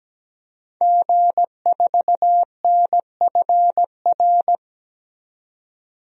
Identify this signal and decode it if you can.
Morse code, heard as a keyed tone, about 17 wpm, 700 Hz. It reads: G4NFR
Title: DXCC-100 Call Signs - 5 Characters - 17wpm